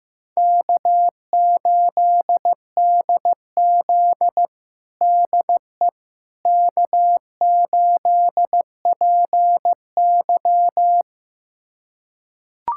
Morse code, heard as a keyed tone, about 15 wpm, 700 Hz. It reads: K8DZ DE K8PY
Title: USA Callsign DE Callsign - 4 Characters - 15wpm